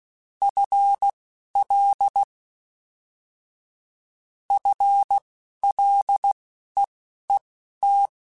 Morse code, heard as a keyed tone, about 16 wpm, spaced as for 12 wpm, 800 Hz. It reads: FL FLEET
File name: Combinations / Combinations-mp3s / Combinations@10